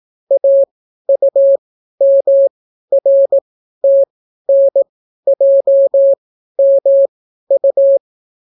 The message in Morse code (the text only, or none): AUMRTNJMU